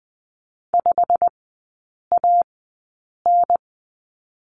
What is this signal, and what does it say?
Morse code, heard as a keyed tone, about 20 wpm, spaced as for 8 wpm, 700 Hz. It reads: 5AN
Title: 100 Lesson 17e - Letter u - Three Characters 20wpm